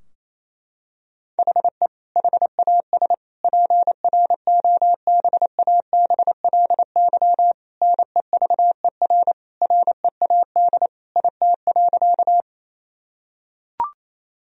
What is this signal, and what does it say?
Morse code, heard as a keyed tone, about 28 wpm, 700 Hz. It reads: HE HAS PROBABLY NEVER READ IT.
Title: Sentences from Top 1000 Words - Review Part 4 - 28wpm